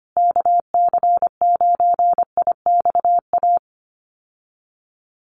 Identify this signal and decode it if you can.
Morse code, heard as a keyed tone, about 25 wpm, 700 Hz. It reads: KC9IXA
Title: USA Call Signs Repeated 3x 25wpm